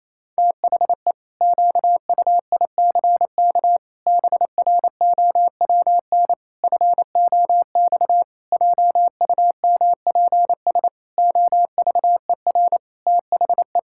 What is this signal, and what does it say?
Morse code, heard as a keyed tone, about 28 wpm, 700 Hz. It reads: THE QUICK BROWN FOX JUMPS OVER THE
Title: Pangram Challenge v2 - 28wpm